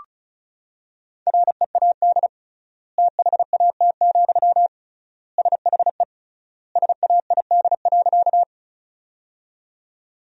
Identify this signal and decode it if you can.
Morse code, heard as a keyed tone, about 36 wpm, 700 Hz. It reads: READ THAT, SHE SAID.
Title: 3x Word-Spacing - Sentences from Top 200 Words - 35wpm